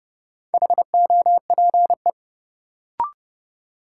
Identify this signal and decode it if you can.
Morse code, heard as a keyed tone, about 30 wpm, 700 Hz. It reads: HOPE
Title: Sentences from Top 1000 Words - New Word Drill - 30wpm